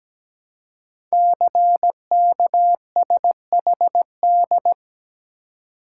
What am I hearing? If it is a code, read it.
Morse code, heard as a keyed tone, about 17 wpm, 700 Hz. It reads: CKSHD
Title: Code Groups - 5 Letters - 17wpm